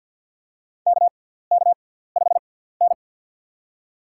Morse code, heard as a keyed tone, about 46 wpm, spaced as for 17 wpm, 700 Hz. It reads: KK5N
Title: USA Call Signs - 4 Characters - 6x Character Spacing - 50wpm